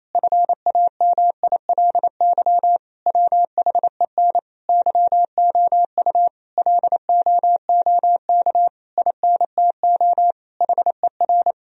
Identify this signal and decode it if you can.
Morse code, heard as a keyed tone, about 28 wpm, 700 Hz. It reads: FAMILY WHEN YOU LOOK INTO HER
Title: Sentences from Top 1000 Words - Review Part 4 - 28wpm